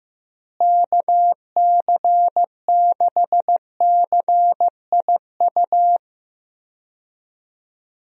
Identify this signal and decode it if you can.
Morse code, heard as a keyed tone, about 15 wpm, 700 Hz. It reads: KC6CIU
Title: USA Call Signs - 6 Characters - Repeated 3x - 15wpm